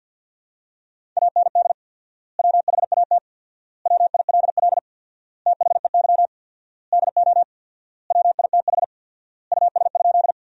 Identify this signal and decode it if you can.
Morse code, heard as a keyed tone, about 50 wpm, 700 Hz. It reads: AND WHAT WILL THEY DO WITH US?
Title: 4x Word-Spacing - Sentences from Top 200 Words - 50wpm